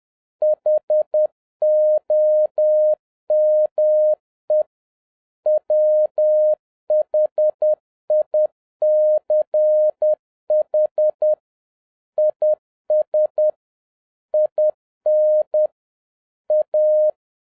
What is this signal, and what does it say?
Morse code, heard as a keyed tone, about 10 wpm, 610 Hz. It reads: HOME WHICH IS IN A
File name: 2011-07-mp3 / n46 JOHN K3TN 10wpm